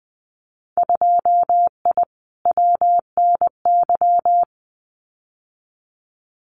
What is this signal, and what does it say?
Morse code, heard as a keyed tone, about 20 wpm, 700 Hz. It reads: 2I WNY
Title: Winter Field Day Exchanges Encore 20wpm